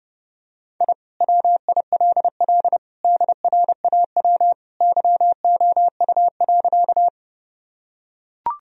Morse code, heard as a keyed tone, about 30 wpm, 700 Hz. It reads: I WILL DRAW YOU.
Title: Sentences from Top 1000 Words - Review Part 1 - 30wpm